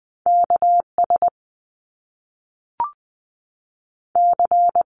Morse code, heard as a keyed tone, about 20 wpm, 700 Hz. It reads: K S C
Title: Top Two Letter Combinations in Words 20wpm